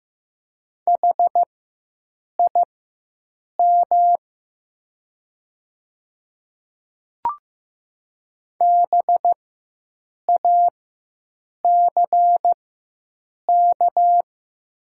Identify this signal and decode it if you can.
Morse code, heard as a keyed tone, about 15 wpm, spaced as for 7 wpm, 700 Hz. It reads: HIM BACK
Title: 4x Character Spacing - Top 300 Words - 15wpm